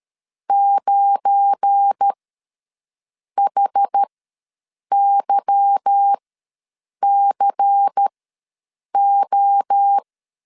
Morse code, heard as a keyed tone, about 13 wpm, spaced as for 7 wpm, 800 Hz. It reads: 9HYCO
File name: LEZIONI / 8 YJ45 / 8 Y ws 12wpm 3409